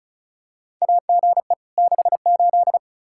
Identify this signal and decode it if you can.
Morse code, heard as a keyed tone, about 34 wpm, 700 Hz. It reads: AGE 68
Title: Practice QSO - Exchange 3 - More Info - 35wpm